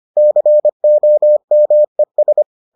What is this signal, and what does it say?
Morse code, heard as a keyed tone, about 25 wpm, 600 Hz. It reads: COMES